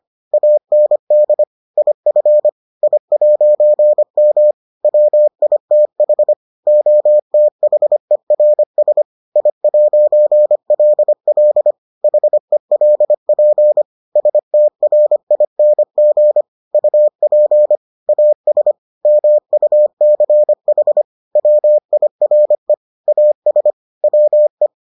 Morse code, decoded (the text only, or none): AND IF I'M WITH OTHERS I'LL HELP STRING UP AS MUCH WIRE AS WE